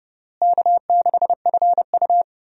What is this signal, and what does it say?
Morse code, heard as a keyed tone, about 32 wpm, 700 Hz. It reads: K6FU
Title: USA Call Signs - 4 Characters - Repeated 3x - 30wpm